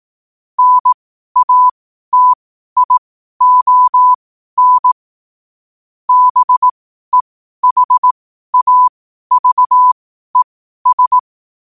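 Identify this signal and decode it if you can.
Morse code, heard as a keyed tone, about 18 wpm, spaced as for 12 wpm, 1000 Hz. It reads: NATION BEHAVES
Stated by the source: Morse practice sent at 12 wpm.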